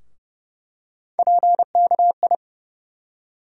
Morse code, heard as a keyed tone, about 30 wpm, 700 Hz. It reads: PKI
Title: USA Call Sign Postfixes - 3 Letters 30wpm